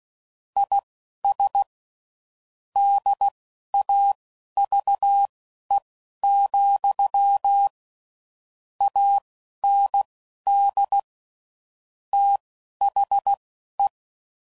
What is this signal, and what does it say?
Morse code, heard as a keyed tone, about 16 wpm, spaced as for 11 wpm, 800 Hz. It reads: IS DAVE, AND THE
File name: CW Code Groups @10WPM / QSOs / QSO kc7haa